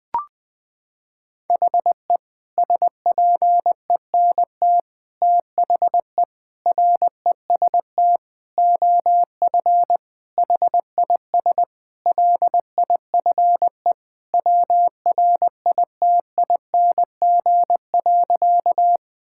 Morse code, heard as a keyed tone, about 20 wpm, 700 Hz. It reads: HE SPENT THE REST OF HIS LIFE WRITING.
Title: Sentences with 5-Word Phrases - 20wpm